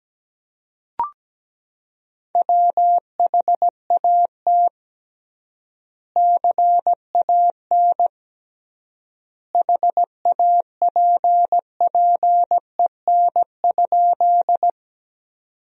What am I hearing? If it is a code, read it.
Morse code, heard as a keyed tone, about 17 wpm, 700 Hz. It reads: WHAT CAN HAPPEN?
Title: 3x Word-Spacing - Sentences from Top 400 Words - 17wpm